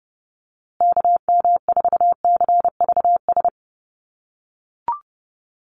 Morse code, heard as a keyed tone, about 29 wpm, 700 Hz. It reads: KM4CVS
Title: USA Call Signs Repeated 3x 30wpm